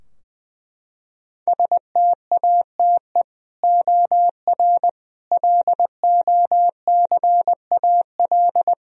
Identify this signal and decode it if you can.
Morse code, heard as a keyed tone, about 20 wpm, 700 Hz. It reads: STATE OR LOCAL
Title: Sentences from Top 1000 Words - Review Part 3 - 20wpm